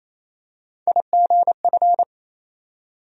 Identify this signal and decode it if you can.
Morse code, heard as a keyed tone, about 28 wpm, 700 Hz. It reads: IGF
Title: USA Call Sign Postfixes - 3 Letters 28wpm